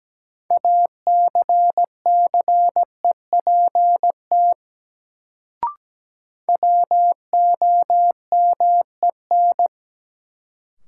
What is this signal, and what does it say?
Morse code, heard as a keyed tone, about 17 wpm, 700 Hz. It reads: ACCEPT WOMEN
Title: Sentences from Top 1300 Words - New Word Drill Alt - 17wpm